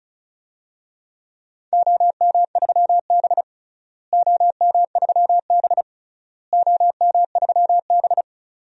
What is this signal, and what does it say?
Morse code, heard as a keyed tone, about 36 wpm, 700 Hz. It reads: OM3B OM3B OM3B
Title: DXCC-100 Call Signs - 4 Characters - Repeated 3x - 35wpm